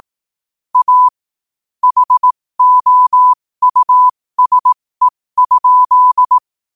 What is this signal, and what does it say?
Morse code, heard as a keyed tone, about 18 wpm, spaced as for 15 wpm, 1000 Hz. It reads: A HOUSE?
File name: qotdfiles / 15wpm